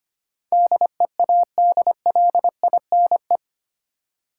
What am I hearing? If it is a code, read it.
Morse code, heard as a keyed tone, about 25 wpm, 700 Hz. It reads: DEADLINE
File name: Eight Letter Words - 25wpm